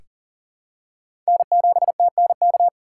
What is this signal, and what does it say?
Morse code, heard as a keyed tone, about 40 wpm, 700 Hz. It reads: N7TNK
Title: USA Call Signs - 2x Character Spacing - 40wpm